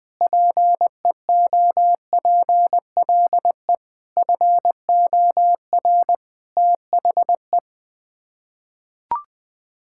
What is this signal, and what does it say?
Morse code, heard as a keyed tone, about 21 wpm, 700 Hz. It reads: PEOPLE FOR THE
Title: Sets of 8 Words - 20wpm